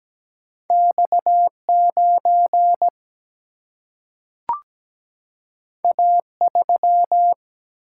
Morse code, heard as a keyed tone, about 17 wpm, 700 Hz. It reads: X 9 A 3
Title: Single-Letter Single-Number 17wpm